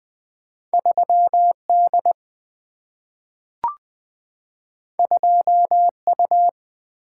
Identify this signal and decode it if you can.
Morse code, heard as a keyed tone, about 20 wpm, 700 Hz. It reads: 3D 2U
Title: Two Letter-Number Combinations - 20wpm